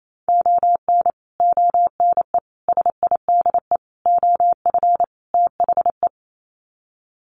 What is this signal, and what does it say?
Morse code, heard as a keyed tone, about 28 wpm, 700 Hz. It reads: ON ONE SIDE OF THE
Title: Sets of 5 Words 28wpm